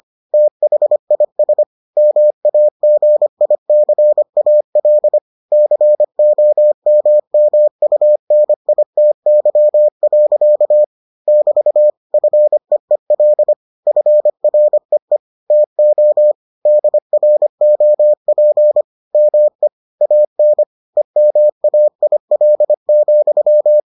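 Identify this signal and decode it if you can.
Morse code, heard as a keyed tone, about 25 wpm, 600 Hz. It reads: THIS MAGICAL COMMUNITY. <BT> FEEL FREE TO DROP ME AN EMAIL,
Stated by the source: Morse edition of the Foundations of Amateur Radio episodes.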